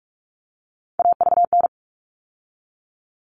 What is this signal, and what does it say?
Morse code, heard as a keyed tone, about 44 wpm, 700 Hz. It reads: AVN
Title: All Cut Numbers - 3-Digit 45wpm